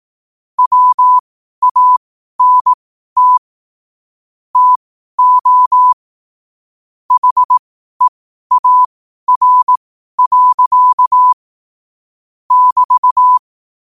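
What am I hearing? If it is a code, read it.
Morse code, heard as a keyed tone, about 18 wpm, spaced as for 12 wpm, 1000 Hz. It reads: WANT TO HEAR. <BT>
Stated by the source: Morse code at 12 wpm.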